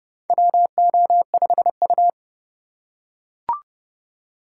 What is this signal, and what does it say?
Morse code, heard as a keyed tone, about 29 wpm, 700 Hz. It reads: WO5U
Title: USA Call Signs - 4 Characters - 3x Character Spacing - 30wpm